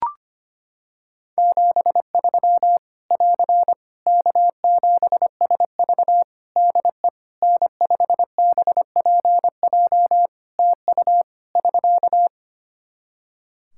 Morse code, heard as a keyed tone, about 25 wpm, 700 Hz. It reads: 73 <AR> K7SV DE N5BPJ TU <SK>
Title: Practice QSO - Exchange 4 - Ending - 25wpm